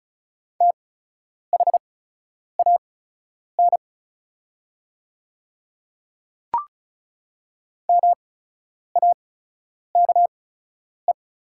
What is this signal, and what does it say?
Morse code, heard as a keyed tone, about 36 wpm, spaced as for 10 wpm, 700 Hz. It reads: THAN MAKE
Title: 8x Character Spacing - Top 100 Words - 35wpm